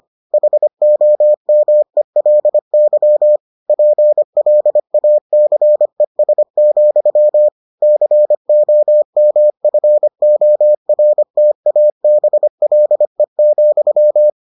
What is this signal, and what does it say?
Morse code, heard as a keyed tone, about 25 wpm, 600 Hz. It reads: HOMELY PLACES, COMFORTABLE,